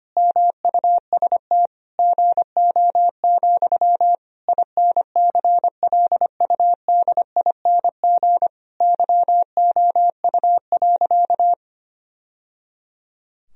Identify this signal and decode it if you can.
Morse code, heard as a keyed tone, about 25 wpm, 700 Hz. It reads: MUST GO, INCLUDING YOU.
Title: Sentences from Top 1000 Words - Review Part 4 - 25wpm